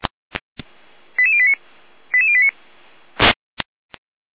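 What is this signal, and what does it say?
Five-tone selective call (ZVEI), heard as quick runs of five tones, tones of about 70 ms. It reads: 88089 88089
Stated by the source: ZVEI five-tone call 88089 received by SDR.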